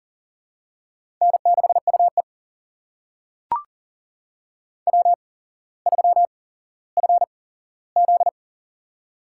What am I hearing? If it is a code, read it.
Morse code, heard as a keyed tone, about 39 wpm, spaced as for 17 wpm, 700 Hz. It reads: N6UE W3FZ
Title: USA Call Signs - 4 Characters - 8x Character Spacing - 40wpm